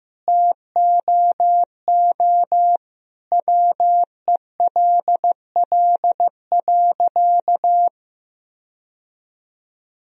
Morse code, heard as a keyed tone, about 15 wpm, 700 Hz. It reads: TOO WELL.
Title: Sentences from Top 1000 Words - Review Part 3 - 15wpm